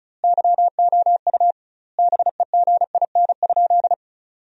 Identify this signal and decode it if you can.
Morse code, heard as a keyed tone, about 36 wpm, 700 Hz. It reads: YOU BEGIN?
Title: 2x Word-Spacing - Sentences from Top 300 Words - 35wpm